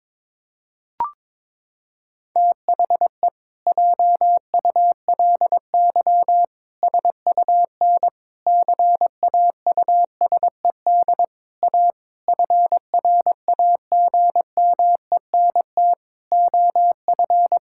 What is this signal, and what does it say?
Morse code, heard as a keyed tone, about 22 wpm, 700 Hz. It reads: THE JULY SUN CAUSED A FRAGMENT OF
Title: Pangram Challenge v2 - 22wpm